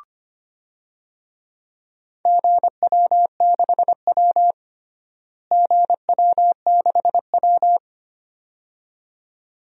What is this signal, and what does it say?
Morse code, heard as a keyed tone, about 25 wpm, 700 Hz. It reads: GW6W GW6W
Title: DXCC-100 Call Signs - 4 Characters - Repeated 2x - 25wpm